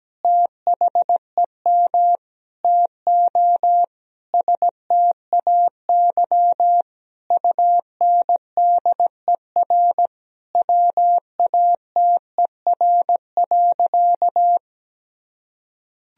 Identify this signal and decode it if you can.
Morse code, heard as a keyed tone, about 17 wpm, 700 Hz. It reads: THEM TO STAY UNDER WATER.
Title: Sentences from Top 1000 Words - Review Part 2 - 17wpm